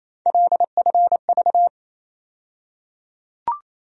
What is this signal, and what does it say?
Morse code, heard as a keyed tone, about 28 wpm, 700 Hz. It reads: LFV
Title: USA Call Sign Postfixes - 3 Letters 28wpm